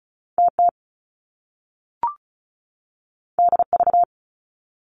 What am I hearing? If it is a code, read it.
Morse code, heard as a keyed tone, about 36 wpm, 700 Hz. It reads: TT DV
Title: Top Two Letter Combinations in Words 35wpm